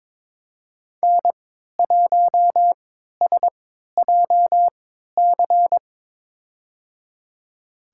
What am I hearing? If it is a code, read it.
Morse code, heard as a keyed tone, about 22 wpm, spaced as for 13 wpm, 700 Hz. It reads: N1SJC
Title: USA Call Signs - 5 Characters - 3x Character Spacing - 22wpm